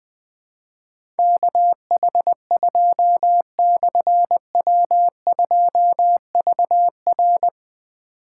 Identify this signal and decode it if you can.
Morse code, heard as a keyed tone, about 20 wpm, 700 Hz. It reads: KH2/W2VR
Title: Difficult Call Signs 20wpm